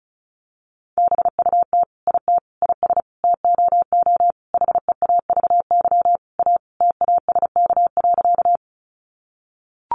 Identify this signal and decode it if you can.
Morse code, heard as a keyed tone, about 34 wpm, 700 Hz. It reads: BUT IT IS TOO HEAVY A TASK.
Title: Sentences from Top 1100 Words 35wpm